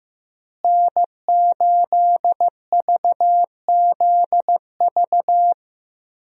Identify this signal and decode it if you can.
Morse code, heard as a keyed tone, about 15 wpm, 700 Hz. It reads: N8VZV
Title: USA Call Signs - 5 Characters - Repeated 3x - 15wpm